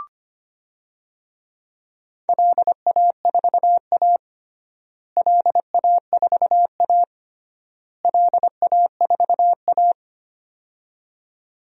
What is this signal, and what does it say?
Morse code, heard as a keyed tone, about 25 wpm, 700 Hz. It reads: LA4A LA4A LA4A
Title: DXCC-100 Call Signs - 4 Characters - Repeated 3x - 25wpm